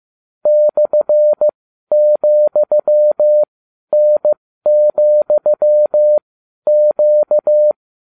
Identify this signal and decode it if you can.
Morse code, heard as a keyed tone, about 15 wpm, spaced as for 11 wpm, 600 Hz. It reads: /,N,Q